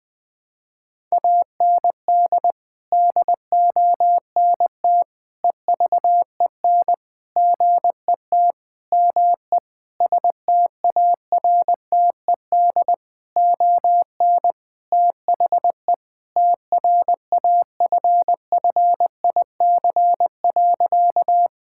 Morse code, heard as a keyed tone, about 19 wpm, 700 Hz. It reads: AND DONT EVEN GET ME STARTED ON THE TRAFFIC.
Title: Sentences with 8-Word Phrases - Spoken Beginning - 20wpm